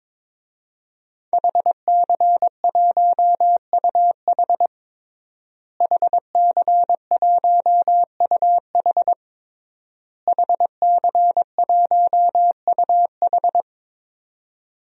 Morse code, heard as a keyed tone, about 22 wpm, 700 Hz. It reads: HC1UH HC1UH HC1UH
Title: DXCC-100 Call Signs - 5 Characters - Repeated 3x - 22wpm